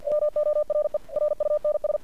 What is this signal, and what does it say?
Morse code, heard as a keyed tone, about 48 wpm, 600 Hz. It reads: MORE RUNS
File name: archived / CRSnet